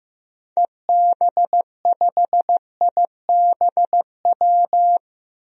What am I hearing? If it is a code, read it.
Morse code, heard as a keyed tone, about 15 wpm, 700 Hz. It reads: EB5IBW
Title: DXCC-100 Call Signs - 6 Characters - 15wpm